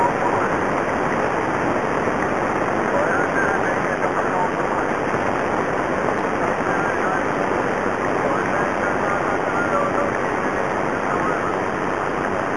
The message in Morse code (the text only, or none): none